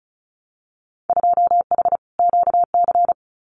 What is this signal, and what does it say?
Morse code, heard as a keyed tone, about 36 wpm, 700 Hz. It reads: 2H QC
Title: Winter Field Day Exchanges 35wpm